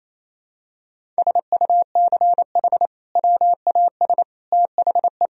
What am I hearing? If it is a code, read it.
Morse code, heard as a keyed tone, about 28 wpm, 700 Hz. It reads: SUCH WAS THE
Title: Sentences from Top 1000 Words - Review Part 3 - 28wpm